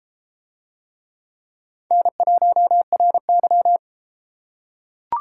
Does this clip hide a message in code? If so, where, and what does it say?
Yes: Morse code from 1.9 to 3.8 s, heard as a keyed tone, about 32 wpm, 700 Hz. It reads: N1RY